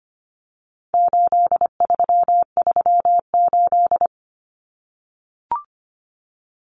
Morse code, heard as a keyed tone, about 25 wpm, 700 Hz. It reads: 8338
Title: Numbers - 4-Digit 25wpm